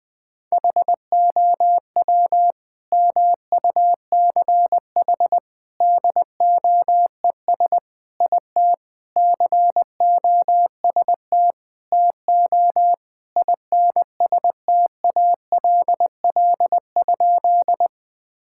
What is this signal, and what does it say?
Morse code, heard as a keyed tone, about 21 wpm, 700 Hz. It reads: HOW MUCH DOES IT COST TO INSTALL?
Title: Sentences from Top 1600 Words 20wpm